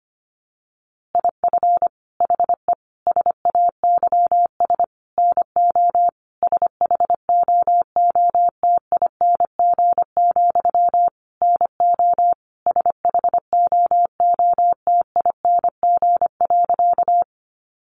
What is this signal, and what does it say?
Morse code, heard as a keyed tone, about 25 wpm, 700 Hz. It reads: IF HE SAYS NO SHOOTING, NO SHOOTING.
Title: A Sound Of Thunder 25wpm